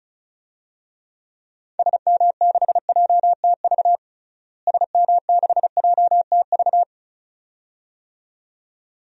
Morse code, heard as a keyed tone, about 34 wpm, 700 Hz. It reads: SM6JTV SM6JTV
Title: DXCC-100 Call Signs - 6 Characters - Repeated 2x - 35wpm